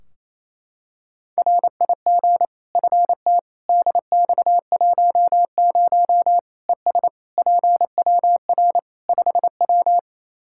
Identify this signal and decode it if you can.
Morse code, heard as a keyed tone, about 28 wpm, 700 Hz. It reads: RIG FT DX10 ES PWR 5W
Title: Practice QSO - Exchange 2 - Info - 28wpm